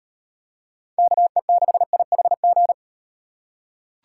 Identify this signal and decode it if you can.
Morse code, heard as a keyed tone, about 38 wpm, 700 Hz. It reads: KE6IHG